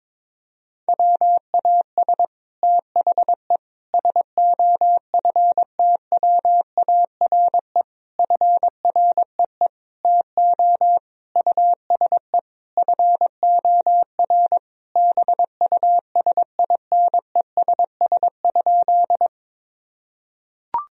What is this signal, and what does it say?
Morse code, heard as a keyed tone, about 22 wpm, 700 Hz. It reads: WAS THE SOFTWARE FREE TO USE FOR BUSINESS?
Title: Sentences from Top 1000 Words - Review Part 3 - 22wpm